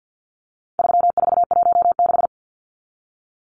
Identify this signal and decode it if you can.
Morse code, heard as a keyed tone, about 50 wpm, 700 Hz. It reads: 3416
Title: Numbers - 4-Digit 50wpm